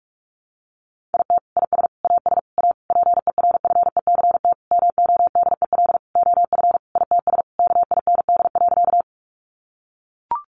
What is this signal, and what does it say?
Morse code, heard as a keyed tone, about 44 wpm, 700 Hz. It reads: IT IS AS A PERFECT MODEL OF ITS KIND.
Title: Sentences from Top 1000 Words 45wpm